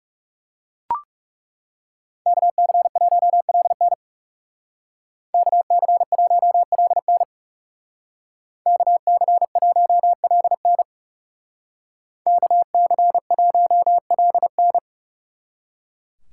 Morse code, heard as a keyed tone, about 37 wpm, 700 Hz. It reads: KC1LN KC1LN KC1LN KC1LN
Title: Speed Racing - USA 2x2 Call Signs 30wpm